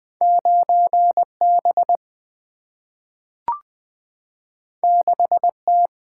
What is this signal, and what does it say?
Morse code, heard as a keyed tone, about 20 wpm, 700 Hz. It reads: 9 B 6 T
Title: Single-Letter Single-Number 20wpm